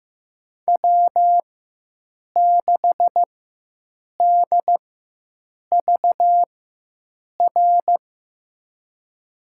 Morse code, heard as a keyed tone, about 15 wpm, spaced as for 7 wpm, 700 Hz. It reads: W6DVR
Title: USA Call Signs - 4x Character Spacing - 15wpm